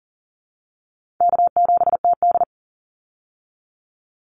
Morse code, heard as a keyed tone, about 40 wpm, 700 Hz. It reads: K7TD
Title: Easy Call Signs - 40wpm